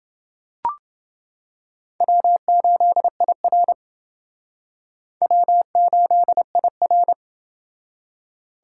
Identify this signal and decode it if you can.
Morse code, heard as a keyed tone, about 28 wpm, 700 Hz. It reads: W8IR W8IR